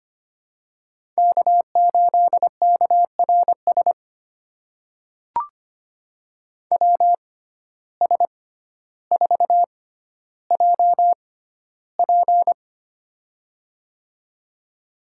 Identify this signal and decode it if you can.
Morse code, heard as a keyed tone, about 25 wpm, spaced as for 13 wpm, 700 Hz. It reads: K8KRS WS4JP
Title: USA Call Signs - 5 Characters - 6x Character Spacing - 25wpm